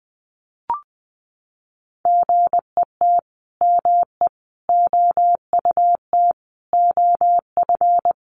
Morse code, heard as a keyed tone, about 20 wpm, 700 Hz. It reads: GET ME OUT OF
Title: A Sound Of Thunder 20wpm